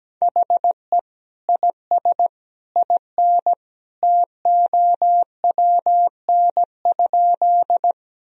Morse code, heard as a keyed tone, about 17 wpm, 700 Hz. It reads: HE IS IN TOWN?
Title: Sentences from Top 1000 Words - Review Part 2 - 17wpm